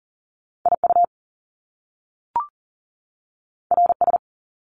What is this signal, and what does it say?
Morse code, heard as a keyed tone, about 40 wpm, 700 Hz. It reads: IU RS
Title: Top Two Letter Combinations in Words 40wpm